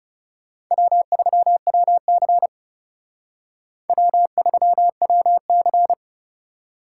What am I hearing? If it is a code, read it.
Morse code, heard as a keyed tone, about 36 wpm, 700 Hz. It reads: W3WC W3WC
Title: Speed Racing - USA 1x2 Call Signs 30wpm